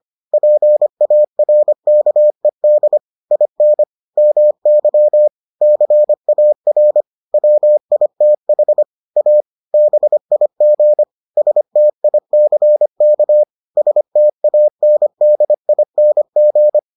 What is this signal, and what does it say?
Morse code, heard as a keyed tone, about 25 wpm, 600 Hz. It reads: PARKED IN MY CAR WITH A BIG STICK STANDING